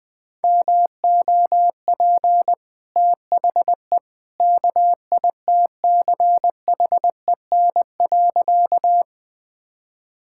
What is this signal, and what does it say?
Morse code, heard as a keyed tone, about 20 wpm, 700 Hz. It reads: MOP THE KITCHEN.
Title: Easy Sentences with Hints - 20wpm